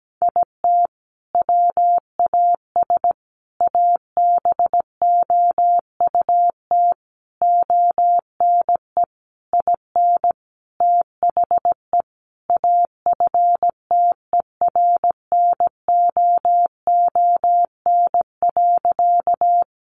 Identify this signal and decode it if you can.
Morse code, heard as a keyed tone, about 17 wpm, 700 Hz. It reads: IT WAS ABOUT ONE IN THE AFTERNOON.
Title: Sentences from Top 1500 Words 17wpm